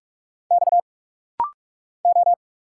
Morse code, heard as a keyed tone, about 46 wpm, 700 Hz. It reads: X O
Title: Single Letters - Rapid-Fire 45wpm